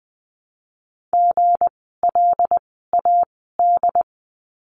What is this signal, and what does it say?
Morse code, heard as a keyed tone, about 20 wpm, spaced as for 14 wpm, 700 Hz. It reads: GLAD
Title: 160 Lesson 26f - Letter g - Words 20wpm